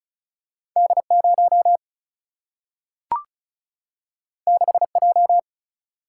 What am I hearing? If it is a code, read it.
Morse code, heard as a keyed tone, about 36 wpm, 700 Hz. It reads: D 0 6 J
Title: Single-Letter Single-Number 35wpm